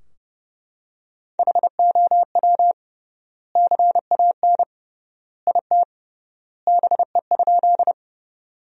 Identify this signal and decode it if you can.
Morse code, heard as a keyed tone, about 32 wpm, 700 Hz. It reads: HOW CAN IT BE?
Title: 3x Word-Spacing - Sentences from Top 100 Words - 30wpm